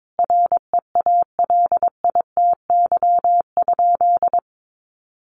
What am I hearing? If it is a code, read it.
Morse code, heard as a keyed tone, about 22 wpm, 700 Hz. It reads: REALITY?
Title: Sentences from Top 1200 Words 22wpm